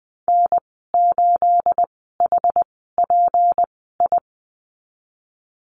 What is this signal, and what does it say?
Morse code, heard as a keyed tone, about 19 wpm, spaced as for 14 wpm, 700 Hz. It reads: N8HPI
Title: 219 Lesson 35g - Number 0 - Call Signs 20wpm